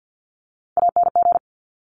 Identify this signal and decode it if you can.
Morse code, heard as a keyed tone, about 50 wpm, 700 Hz. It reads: ANG
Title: Top Three Letter Combinations in Words 50wpm